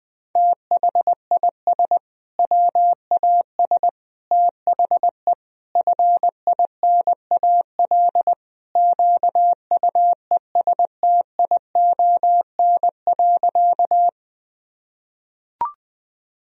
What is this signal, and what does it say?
Morse code, heard as a keyed tone, about 21 wpm, 700 Hz. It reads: THIS WAS THE FINAL QUESTION.
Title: Sentences from Top 1000 Words - Review Part 2 - 20wpm